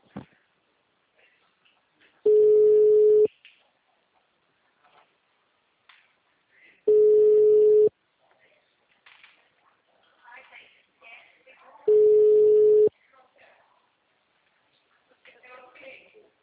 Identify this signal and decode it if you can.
Ringback tone (Europe, 425 Hz), with speech.
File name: out- -2006- - - .0